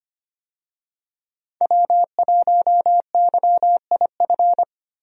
Morse code, heard as a keyed tone, about 25 wpm, 700 Hz. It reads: W1YIF
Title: USA Call Signs - 5 Characters - 25wpm